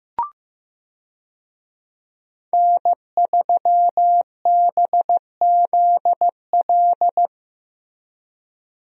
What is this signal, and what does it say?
Morse code, heard as a keyed tone, about 15 wpm, 700 Hz. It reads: N3BZL
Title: USA Call Signs - 5 Characters - Repeated 3x - 15wpm